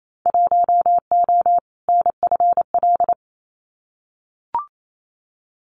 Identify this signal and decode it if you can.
Morse code, heard as a keyed tone, about 28 wpm, 700 Hz. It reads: 1O NFL
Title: Winter Field Day Exchanges 28wpm